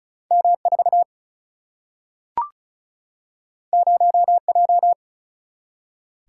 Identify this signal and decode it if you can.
Morse code, heard as a keyed tone, about 36 wpm, 700 Hz. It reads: M4 0J
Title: Single-Letter Single-Number 35wpm